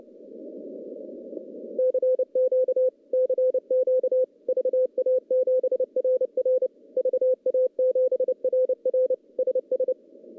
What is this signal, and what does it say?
Morse code, heard as a keyed tone, about 30 wpm, 500 Hz. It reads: CQ CQ VA7RR VA7RR SS
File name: contests / CW-SS-k5tr- -0900